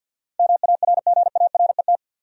Morse code, heard as a keyed tone, about 47 wpm, 700 Hz. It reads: MARGARET
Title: USA Common Names - 50wpm